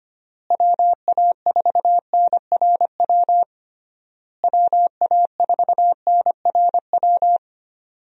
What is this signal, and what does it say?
Morse code, heard as a keyed tone, about 25 wpm, 700 Hz. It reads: WA4NRW WA4NRW
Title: USA Call Signs - 6 Characters - Repeated 3x - 25wpm